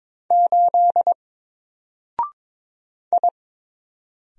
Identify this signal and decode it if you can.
Morse code, heard as a keyed tone, about 22 wpm, 700 Hz. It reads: 8 I
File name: Single-Letter-Number-22wpm